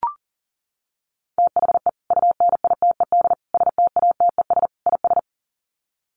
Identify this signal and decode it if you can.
Morse code, heard as a keyed tone, about 40 wpm, 700 Hz. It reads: THE UNITED STATES IS
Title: Sets of 4 Words 40wpm